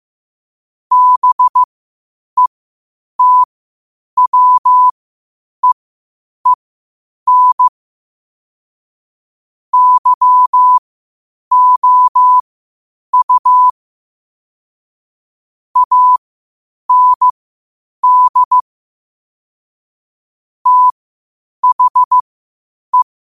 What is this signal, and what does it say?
Morse code, heard as a keyed tone, about 15 wpm, spaced as for 8 wpm, 1000 Hz. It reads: BETWEEN YOU AND THE
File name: qotdfiles / 8wpm